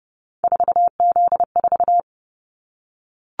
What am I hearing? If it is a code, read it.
Morse code, heard as a keyed tone, about 31 wpm, 700 Hz. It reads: 4Z4
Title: Callsign Prefixes for Top 50 DXCC - 30wpm